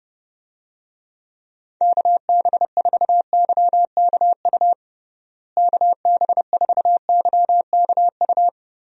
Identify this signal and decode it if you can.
Morse code, heard as a keyed tone, about 29 wpm, 700 Hz. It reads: KB4YKU KB4YKU
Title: USA Call Signs - 6 Characters - Repeated 2x - 30wpm